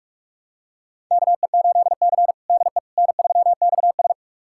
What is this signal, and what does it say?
Morse code, heard as a keyed tone, about 46 wpm, 700 Hz. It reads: KE8C DE N3XS
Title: USA Callsign DE Callsign - 4 Characters - 45wpm